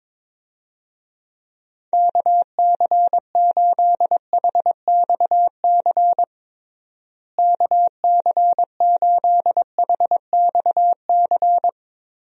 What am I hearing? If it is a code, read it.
Morse code, heard as a keyed tone, about 22 wpm, 700 Hz. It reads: KC8HXC KC8HXC
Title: USA Call Signs - 6 Characters - Repeated 2x - 22wpm